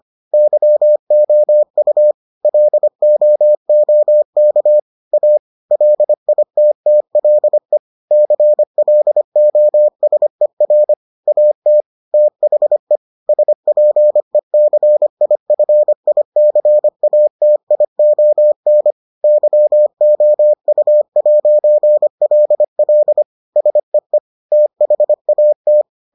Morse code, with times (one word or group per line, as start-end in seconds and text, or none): 0.3-2.1 YOU
2.5-4.8 LOOK
5.1-5.4 A
5.7-7.8 LITTLE
8.1-10.9 CLOSER
11.3-11.8 AT
12.1-13.0 THE
13.3-18.9 SPECIFICATION
19.2-23.2 YOU'LL
23.6-24.2 SEE
24.5-25.8 THAT